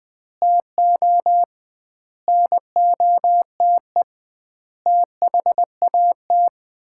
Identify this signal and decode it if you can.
Morse code, heard as a keyed tone, about 20 wpm, 700 Hz. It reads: TO NOTE THAT
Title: 2x Word-Spacing - Sets of 5 Words - 20wpm